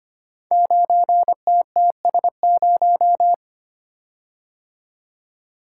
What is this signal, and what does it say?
Morse code, heard as a keyed tone, about 25 wpm, 700 Hz. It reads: 9TTS0
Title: Code Groups - 5 Letter:Numbers - 25wpm